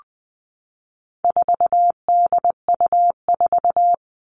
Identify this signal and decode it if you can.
Morse code, heard as a keyed tone, about 20 wpm, 700 Hz. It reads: 4DU4
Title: All Cut Numbers - 4-Digit 20wpm